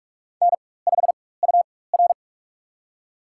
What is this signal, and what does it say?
Morse code, heard as a keyed tone, about 46 wpm, spaced as for 21 wpm, 700 Hz. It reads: N5UR
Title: USA Call Signs - 4 Characters - 4x Character Spacing - 45wpm